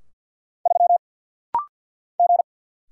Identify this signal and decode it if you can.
Morse code, heard as a keyed tone, about 50 wpm, 700 Hz. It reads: 3 G
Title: Single Letter-Number Alt - Rapid-Fire 50wpm